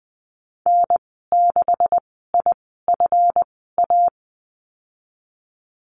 Morse code, heard as a keyed tone, about 19 wpm, spaced as for 14 wpm, 700 Hz. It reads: N6IFA
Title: 161 Lesson 26g - Letter g - Call Signs 20wpm